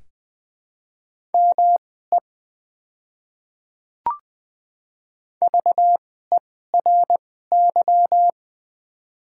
Morse code, heard as a keyed tone, about 20 wpm, spaced as for 15 wpm, 690 Hz. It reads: ME VERY
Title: 2x Character Spacing - Top 500 Words - 20wpm